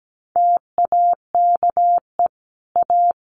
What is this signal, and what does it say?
Morse code, heard as a keyed tone, about 17 wpm, 700 Hz. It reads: TAKE A
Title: Sets of 5 Words Encore2 17wpm